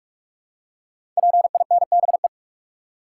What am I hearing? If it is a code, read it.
Morse code, heard as a keyed tone, about 44 wpm, 700 Hz. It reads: PINBE
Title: Code Groups - 5 Letter:Numbers - 45wpm